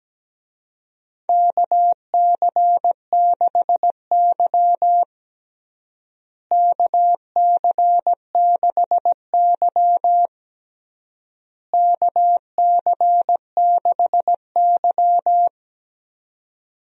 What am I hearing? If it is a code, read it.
Morse code, heard as a keyed tone, about 17 wpm, 700 Hz. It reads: KC6Y KC6Y KC6Y
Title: USA Call Signs - 4 Characters - Repeated 3x - 17wpm